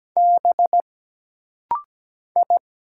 Morse code, heard as a keyed tone, about 17 wpm, 700 Hz. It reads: B I
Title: Single Letters - Rapid-Fire 17wpm